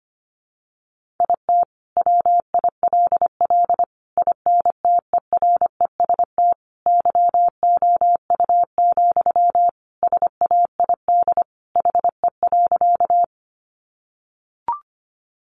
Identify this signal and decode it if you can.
Morse code, heard as a keyed tone, about 25 wpm, 700 Hz. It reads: IT WILL INTEREST YOU, SAID HE.
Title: Sentences from Top 400 Words 25wpm